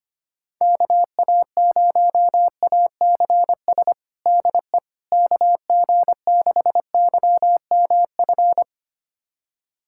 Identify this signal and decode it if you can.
Morse code, heard as a keyed tone, about 25 wpm, 700 Hz. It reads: KA0ACS DE KG6YMF
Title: USA Callsign DE Callsign 25wpm